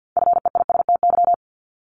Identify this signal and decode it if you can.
Morse code, heard as a keyed tone, about 50 wpm, 700 Hz. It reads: FEISTY
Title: Multi-letter Phonemes 50wpm